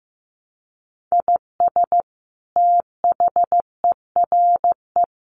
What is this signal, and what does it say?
Morse code, heard as a keyed tone, about 15 wpm, 700 Hz. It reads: IS THERE
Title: Sentences from Top 1300 Words 15wpm